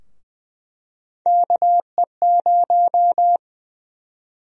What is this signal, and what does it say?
Morse code, heard as a keyed tone, about 20 wpm, 700 Hz. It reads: KE0
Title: USA Call Sign Prefixes - 20wpm